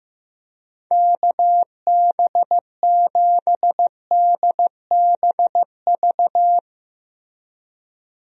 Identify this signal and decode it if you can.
Morse code, heard as a keyed tone, about 15 wpm, 700 Hz. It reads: KB7DBV
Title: USA Call Signs - 6 Characters - Repeated 2x - 15wpm